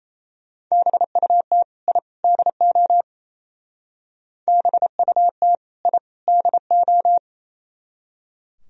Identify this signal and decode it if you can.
Morse code, heard as a keyed tone, about 34 wpm, 700 Hz. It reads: BUT I DO BUT I DO
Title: Speed-Racing - Sets of 3 Words 28wpm